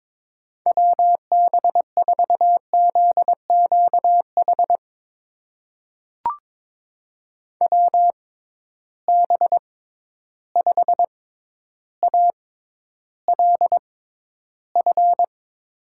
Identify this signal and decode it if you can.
Morse code, heard as a keyed tone, about 22 wpm, spaced as for 11 wpm, 700 Hz. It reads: WB4ZQH WB5ALF
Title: USA Call Signs - 6 Characters - 6x Character Spacing - 22wpm